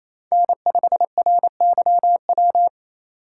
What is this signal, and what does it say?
Morse code, heard as a keyed tone, about 28 wpm, 700 Hz. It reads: N5RYW
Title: USA Call Signs - 5 Characters - 2x Character Spacing - 28wpm